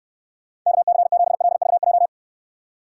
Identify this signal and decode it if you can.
Morse code, heard as a keyed tone, about 69 wpm, 700 Hz. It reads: KC7LVJ